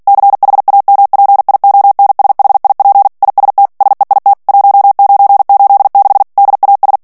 Morse code, heard as a keyed tone, about 48 wpm, 800 Hz. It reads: CHAMPIONSHIP IST SEIT 1986 DAS
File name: morse mp3 / ALLEMAND texte clair 2 vitesse 45 mots minute